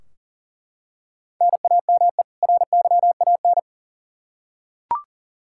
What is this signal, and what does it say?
Morse code, heard as a keyed tone, about 40 wpm, 700 Hz. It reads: NAME RYAN
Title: Practice QSO - Exchange 1 - Beginning - 40wpm